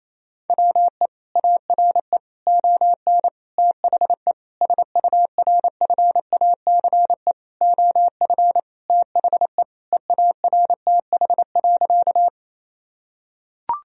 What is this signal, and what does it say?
Morse code, heard as a keyed tone, about 28 wpm, 700 Hz. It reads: WE ARE ON THE SURFACE OF THE EARTH.
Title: Sentences from Top 1000 Words - Review Part 3 - 28wpm